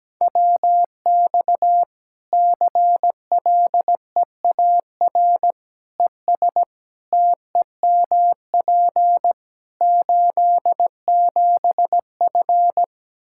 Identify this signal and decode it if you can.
Morse code, heard as a keyed tone, about 17 wpm, 700 Hz. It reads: WX CLEAR ES TEMP 87F
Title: Practice QSO - Exchange 2 - Info - 17wpm